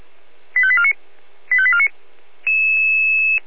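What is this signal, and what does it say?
Five-tone selective call (ZVEI), heard as quick runs of five tones, tones of about 70 ms. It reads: 85849 85849
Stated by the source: Received on an SDR.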